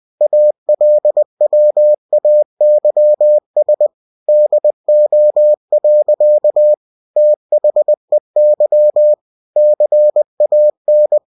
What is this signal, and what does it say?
Morse code, heard as a keyed tone, about 20 wpm, 600 Hz. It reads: ALWAYS DO. THEY CAN
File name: cw-books / winnie-the-pooh / winnie-20wpm-page-0079